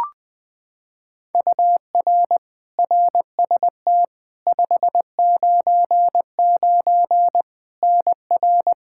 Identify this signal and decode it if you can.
Morse code, heard as a keyed tone, about 20 wpm, 700 Hz. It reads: UR RST 599 NR
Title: Practice QSO - Exchange 1 - Beginning - 20wpm